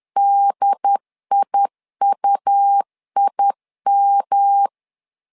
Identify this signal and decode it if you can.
Morse code, heard as a keyed tone, about 10 wpm, 800 Hz. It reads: DIUIM
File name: LEZIONI / 3 MWRO / 3 MW 10wpm 1959